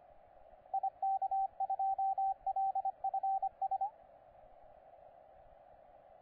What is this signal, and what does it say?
Morse code, heard as a keyed tone, about 25 wpm, 780 Hz, with radio noise. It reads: IK2LFS
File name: posts / 9l1jt-15cw